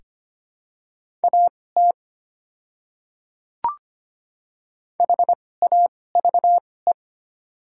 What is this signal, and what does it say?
Morse code, heard as a keyed tone, about 25 wpm, spaced as for 18 wpm, 690 Hz. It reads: AT HAVE
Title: 2x Character Spacing - Top 100 Words - 25wpm